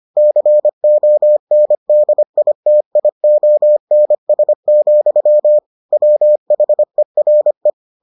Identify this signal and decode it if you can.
Morse code, heard as a keyed tone, about 25 wpm, 600 Hz. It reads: CONDITIONS, WHERE